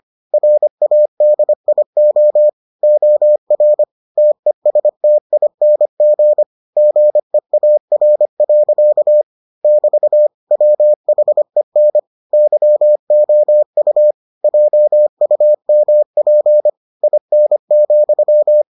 Morse code, heard as a keyed tone, about 25 wpm, 600 Hz. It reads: RADIO OR TESTING GEAR. <BT> WHEN YOU JUMP IN,